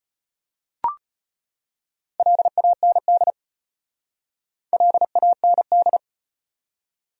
Morse code, heard as a keyed tone, about 38 wpm, 700 Hz. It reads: LAND LAND